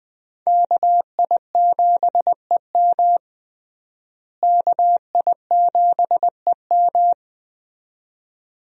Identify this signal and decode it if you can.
Morse code, heard as a keyed tone, about 20 wpm, 700 Hz. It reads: KI7EM KI7EM
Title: USA Call Signs - 5 Characters - Repeated 2x - 20wpm